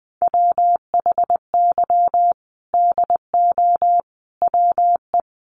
Morse code, heard as a keyed tone, about 20 wpm, 700 Hz. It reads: WHY DO WE
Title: Sentences from Top 400 Words 20wpm